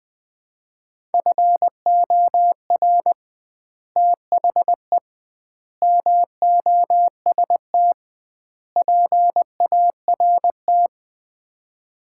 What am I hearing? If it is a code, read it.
Morse code, heard as a keyed tone, about 20 wpm, 700 Hz. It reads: FOR THE MOST PART
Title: 2x Word-Spacing - Sets of 4 Words - 20wpm